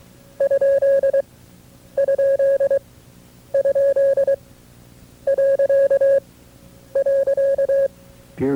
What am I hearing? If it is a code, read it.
Morse code, heard as a keyed tone, about 23 wpm, spaced as for 9 wpm, 550 Hz. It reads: ???..
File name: FreeFiles / CW-practice / KB6MT-tape2-side2